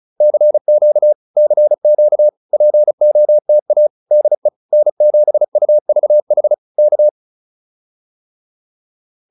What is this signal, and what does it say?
Morse code, heard as a keyed tone, about 34 wpm, 600 Hz. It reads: CQ CQ POTA DE N7UVH K